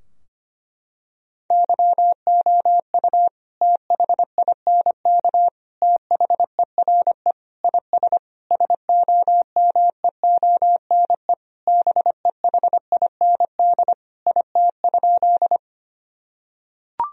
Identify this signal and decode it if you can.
Morse code, heard as a keyed tone, about 25 wpm, 700 Hz. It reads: YOU THINK THERE IS SOMEONE BEHIND IT?
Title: Sentences from Top 1000 Words - Review Part 4 - 25wpm